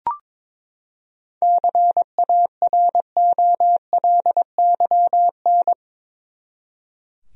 Morse code, heard as a keyed tone, about 22 wpm, 700 Hz. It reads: CAROLYN
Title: USA Common Names - 22wpm